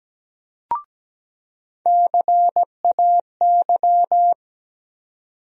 Morse code, heard as a keyed tone, about 17 wpm, 700 Hz. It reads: CAY
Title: USA Call Sign Postfixes - 3 Letters 17wpm